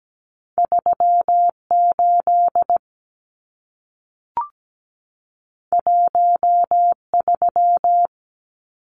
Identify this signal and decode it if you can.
Morse code, heard as a keyed tone, about 17 wpm, 700 Hz. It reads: 38 13
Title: Numbers - 2-Digit 17wpm